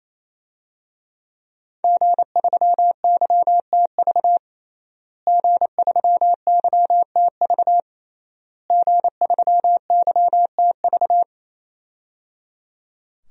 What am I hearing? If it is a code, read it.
Morse code, heard as a keyed tone, about 28 wpm, 700 Hz. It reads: G3YTV G3YTV G3YTV
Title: DXCC-100 Call Signs - 5 Characters - Repeated 3x - 28wpm